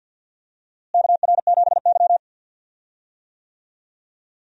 Morse code, heard as a keyed tone, about 53 wpm, 700 Hz. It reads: KR7Y
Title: USA Call Signs - 4 Characters - Repeated 3x - 50wpm